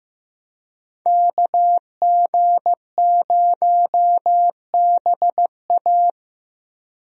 Morse code, heard as a keyed tone, about 15 wpm, 700 Hz. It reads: KG0BA
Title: USA Call Signs - 5 Characters - 15wpm